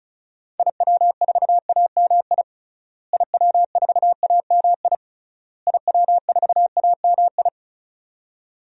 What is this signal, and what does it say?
Morse code, heard as a keyed tone, about 36 wpm, 700 Hz. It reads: IW4AMI IW4AMI IW4AMI
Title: DXCC-100 Call Signs - 6 Characters - Repeated 3x - 35wpm